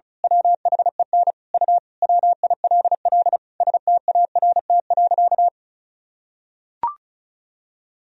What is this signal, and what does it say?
Morse code, heard as a keyed tone, about 36 wpm, 700 Hz. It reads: WHEN U WILL START.
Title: Sentences from Top 300 Words and Abbreviations 35wpm